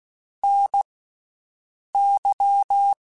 Morse code, heard as a keyed tone, about 16 wpm, 800 Hz. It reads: N Y
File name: Special INGs-mp3s / Special INGs-mp3s / Slow ing @15